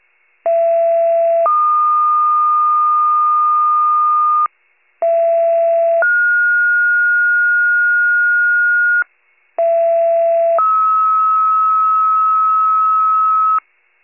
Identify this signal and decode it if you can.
Two-tone page, heard as a one-second tone, then a three-second tone, about 670 then 1220 Hz, about 670 then 1430 Hz, about 670 then 1290 Hz. It is heard from 0.4 to 13.6 s.
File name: pager / 39 FIRE / wcalert